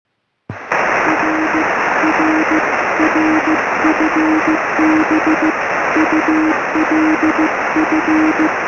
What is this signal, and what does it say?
Morse code, heard as a keyed tone, about 15 wpm, 310 Hz, with noise. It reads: R R R FB ULF
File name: sm7lcb / g4cch 23cm